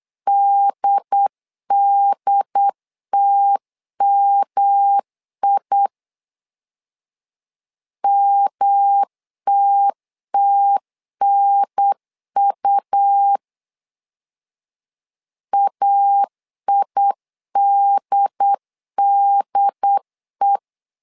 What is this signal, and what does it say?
Morse code, heard as a keyed tone, about 8 wpm, 800 Hz. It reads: DDTMI MTTNU AIDDE
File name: LEZIONI / 3 MWRO / 3 M 8wpm 2300